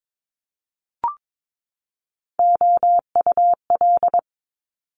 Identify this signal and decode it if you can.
Morse code, heard as a keyed tone, about 22 wpm, 700 Hz. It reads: OUL
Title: Multi-letter Phonemes 22wpm